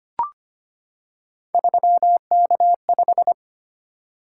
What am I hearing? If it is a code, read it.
Morse code, heard as a keyed tone, about 25 wpm, 700 Hz. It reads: 3K5
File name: Three Letter-Number Combinations - 25wpm